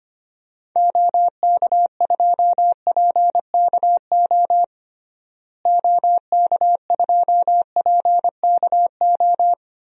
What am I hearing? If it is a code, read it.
Morse code, heard as a keyed tone, about 25 wpm, 700 Hz. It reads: OK2PKO OK2PKO
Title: DXCC-100 Call Signs - 6 Characters - Repeated 2x - 25wpm